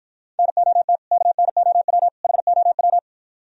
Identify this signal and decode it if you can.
Morse code, heard as a keyed tone, about 56 wpm, 700 Hz. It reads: NOT KNOW HOW